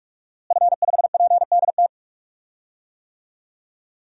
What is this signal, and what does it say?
Morse code, heard as a keyed tone, about 46 wpm, 700 Hz. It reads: F5PDT
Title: DXCC-100 Call Signs - 5 Characters - Repeated 2x - 45wpm